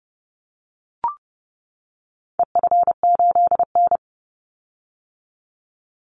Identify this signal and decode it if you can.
Morse code, heard as a keyed tone, about 30 wpm, 700 Hz. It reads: EF8N
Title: Call Signs - 30wpm